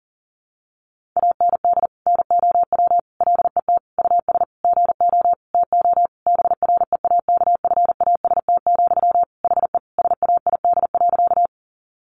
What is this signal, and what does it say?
Morse code, heard as a keyed tone, about 40 wpm, 700 Hz. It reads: AND NOW LET US GO TO BREAKFAST, HE SAID.
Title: Sentences from Top 1500 Words 40wpm